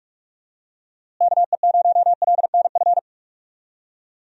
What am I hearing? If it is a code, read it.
Morse code, heard as a keyed tone, about 46 wpm, 700 Hz. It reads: KE0LNF
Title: USA Call Signs - 6 Characters - Repeated 3x - 45wpm